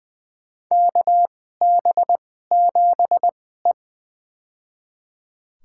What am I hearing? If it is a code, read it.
Morse code, heard as a keyed tone, about 20 wpm, spaced as for 15 wpm, 700 Hz. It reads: KB7E
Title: USA Call Signs - 2x Character Spacing - 20wpm